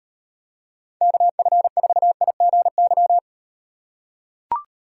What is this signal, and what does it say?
Morse code, heard as a keyed tone, about 38 wpm, 700 Hz. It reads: KF4IGY